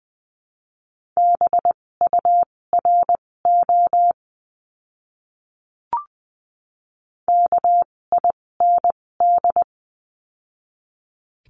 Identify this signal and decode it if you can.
Morse code, heard as a keyed tone, about 20 wpm, spaced as for 16 wpm, 700 Hz. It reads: BURO KIND
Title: 256 Lesson 42a - Words 12-20wpm